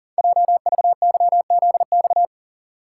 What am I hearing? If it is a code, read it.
Morse code, heard as a keyed tone, about 40 wpm, 690 Hz. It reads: JVYZX
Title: Code Groups - 5 Letters - 40wpm